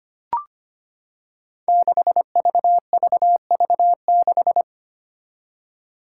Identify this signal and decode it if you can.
Morse code, heard as a keyed tone, about 25 wpm, 700 Hz. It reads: 6VVV6
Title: Five Challenging Letter-Number Combinations 25wpm